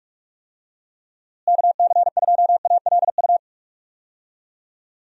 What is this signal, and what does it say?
Morse code, heard as a keyed tone, about 41 wpm, 700 Hz. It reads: KC2ALU
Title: USA Call Signs - 6 Characters - Repeated 2x - 45wpm